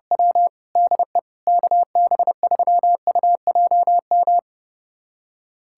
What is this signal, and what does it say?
Morse code, heard as a keyed tone, about 32 wpm, 700 Hz. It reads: W DE KB3UJM
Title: USA Callsign DE Callsign 30wpm